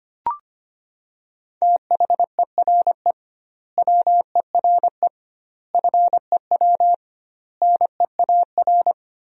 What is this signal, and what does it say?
Morse code, heard as a keyed tone, about 25 wpm, 700 Hz. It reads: THERE WERE FEW NEAR
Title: 2x Word-Spacing - Sentences from Top 300 Words - 25wpm